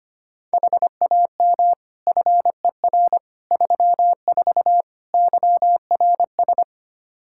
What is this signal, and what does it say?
Morse code, heard as a keyed tone, about 25 wpm, 700 Hz. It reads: HAM FER 34 YRS
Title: Practice QSO - Exchange 3 - More Info - 25wpm